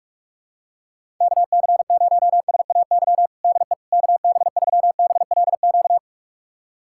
Morse code, heard as a keyed tone, about 45 wpm, 700 Hz. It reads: KC0SAY DE KB3BLQ
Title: USA Callsign DE Callsign - 6 Characters - 45wpm